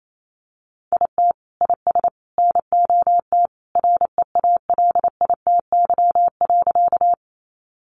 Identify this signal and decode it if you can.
Morse code, heard as a keyed tone, about 28 wpm, 700 Hz. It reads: IT IS NOT REALITY.
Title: Sentences from Top 1200 Words 28wpm